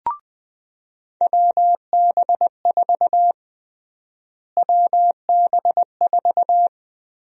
Morse code, heard as a keyed tone, about 20 wpm, 700 Hz. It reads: WB4 WB4
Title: USA Call Sign Prefixes - Repeated 2x - 20wpm